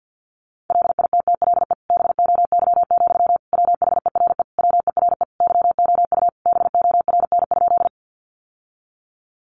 Morse code, heard as a keyed tone, about 49 wpm, 700 Hz. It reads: LITTLE BOY, WHERE WERE YOU BORN?
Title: Sentences from Top 1000 Words 50wpm